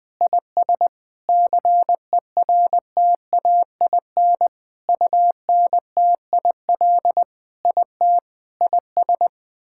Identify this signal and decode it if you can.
Morse code, heard as a keyed tone, about 20 wpm, 700 Hz. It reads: IS CERTAIN UNTIL IT IS
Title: Sentences with 8-Word Phrases - Spoken Beginning - 20wpm